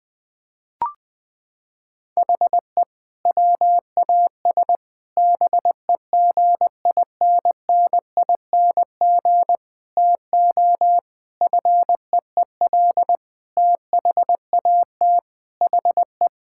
Morse code, heard as a keyed tone, about 20 wpm, 700 Hz. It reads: HE WAS BEGINNING TO FEEL THAT HE
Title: Sentences from Top 1000 Words - Review Part 5 - 20wpm